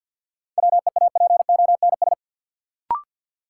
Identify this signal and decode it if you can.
Morse code, heard as a keyed tone, about 51 wpm, 700 Hz. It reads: WEAPONS
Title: Sets of 5 Words Encore - New Word Drill - 50wpm